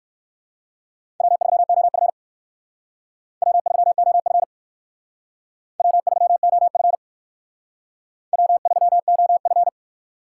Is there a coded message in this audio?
Yes: Morse code from 1.2 to 9.7 s, heard as a keyed tone, about 58 wpm, 700 Hz. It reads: W3OF W3OF W3OF W3OF